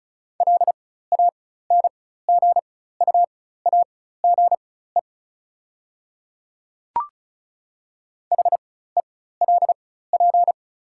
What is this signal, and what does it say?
Morse code, heard as a keyed tone, about 35 wpm, spaced as for 16 wpm, 690 Hz. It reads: LANGUAGE HELP
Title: 4x Character Spacing - Top 500 Words - 35wpm